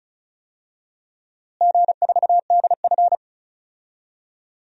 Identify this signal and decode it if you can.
Morse code, heard as a keyed tone, about 36 wpm, 700 Hz. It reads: G4DF
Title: DXCC-100 Call Signs - 4 Characters - Repeated 3x - 35wpm